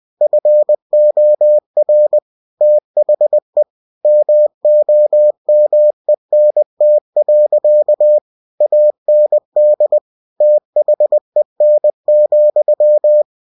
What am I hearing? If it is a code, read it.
Morse code, heard as a keyed tone, about 20 wpm, 600 Hz. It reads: FOR THE MOMENT. AND THEN,
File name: cw-books / winnie-the-pooh / winnie-20wpm-page-0034